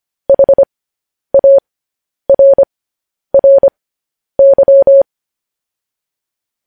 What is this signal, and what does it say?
Morse code, heard as a keyed tone, about 25 wpm, spaced as for 10 wpm, 550 Hz. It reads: HARRY
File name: podcast / News headlines-1x-25x10